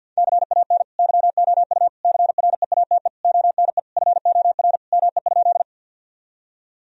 Morse code, heard as a keyed tone, about 50 wpm, 700 Hz. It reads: CAN YOU CREATE ONE FOR ME?
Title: Sentences from Top 1000 Words - Review Part 3 - 50wpm